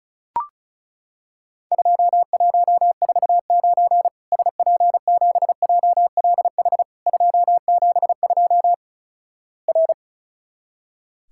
Morse code, heard as a keyed tone, about 36 wpm, 700 Hz. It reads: 2149 SP7JLH 272 R
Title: Worked All Europe DX Contest - QTC Exchange - 35wpm